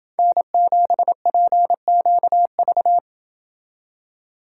Morse code, heard as a keyed tone, about 27 wpm, 700 Hz. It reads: N7PQV